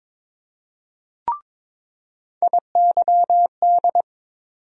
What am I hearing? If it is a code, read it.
Morse code, heard as a keyed tone, about 22 wpm, 700 Hz. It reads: IYD
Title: USA Call Sign Postfixes - 3 Letters 22wpm